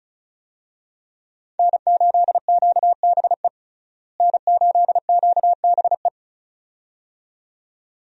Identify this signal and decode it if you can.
Morse code, heard as a keyed tone, about 34 wpm, 700 Hz. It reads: N8QBE N8QBE
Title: USA Call Signs - 5 Characters - Repeated 2x - 35wpm